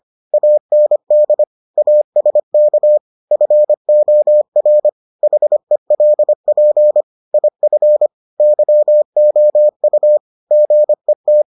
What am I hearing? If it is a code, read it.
Morse code, heard as a keyed tone, about 25 wpm, 600 Hz. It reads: AND ASK FOR HELP IF YOU GET